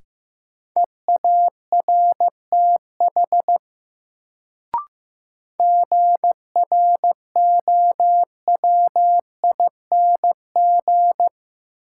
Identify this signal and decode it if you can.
Morse code, heard as a keyed tone, about 15 wpm, 700 Hz. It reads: EARTH GROWING
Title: Sets of 5 Words Encore - New Word Drill - 15wpm